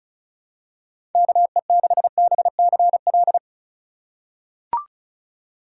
Morse code, heard as a keyed tone, about 36 wpm, 700 Hz. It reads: KE6BCL
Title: USA Call Signs - 6 Characters - 3x Character Spacing - 35wpm